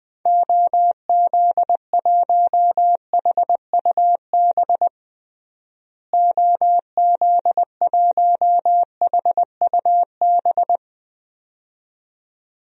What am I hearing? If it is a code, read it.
Morse code, heard as a keyed tone, about 20 wpm, 700 Hz. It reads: OZ1HUB OZ1HUB
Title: DXCC-100 Call Signs - 6 Characters - Repeated 2x - 20wpm